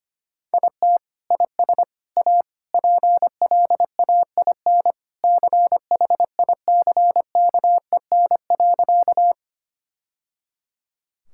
Morse code, heard as a keyed tone, about 25 wpm, 700 Hz. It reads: IT IS A PLAIN CHICKEN.
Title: Sentences from Top 1000 Words - Review Part 5 - 25wpm